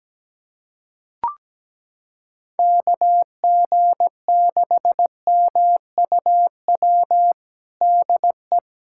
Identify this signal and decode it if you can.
Morse code, heard as a keyed tone, about 17 wpm, 700 Hz. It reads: KG6MUW DE
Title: USA Callsign DE Callsign 17wpm